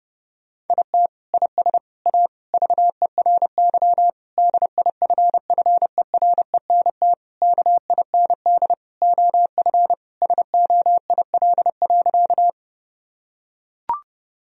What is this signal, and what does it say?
Morse code, heard as a keyed tone, about 30 wpm, 700 Hz. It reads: IT IS A VERY DIFFERENT KIND OF SOIL.
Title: Sentences from Top 1000 Words - Review Part 4 - 30wpm